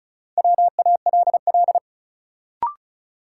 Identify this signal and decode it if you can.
Morse code, heard as a keyed tone, about 34 wpm, 700 Hz. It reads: WALL
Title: Sentences from Top 1100 Words - New Word Drill Alt - 35wpm